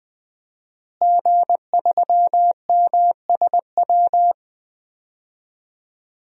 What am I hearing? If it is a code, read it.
Morse code, heard as a keyed tone, about 20 wpm, 700 Hz. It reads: G3MSW
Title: DXCC-100 Call Signs - 5 Characters - Repeated 3x - 20wpm